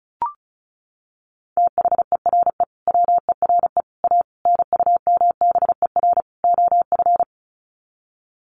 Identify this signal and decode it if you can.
Morse code, heard as a keyed tone, about 34 wpm, 700 Hz. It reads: THERE WERE A NUMBER OF
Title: Sets of 5 Words Encore2 35wpm